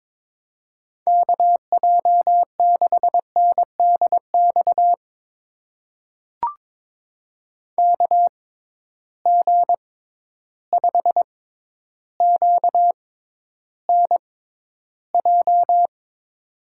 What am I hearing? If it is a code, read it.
Morse code, heard as a keyed tone, about 22 wpm, spaced as for 11 wpm, 700 Hz. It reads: KJ6NDX KG5QNJ
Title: USA Call Signs - 6 Characters - 6x Character Spacing - 22wpm